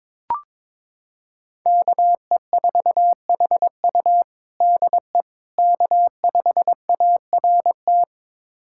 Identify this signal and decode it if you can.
Morse code, heard as a keyed tone, about 22 wpm, 700 Hz. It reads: KE4HU DE K5ART
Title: USA Callsign DE Callsign - 5 Characters - 22wpm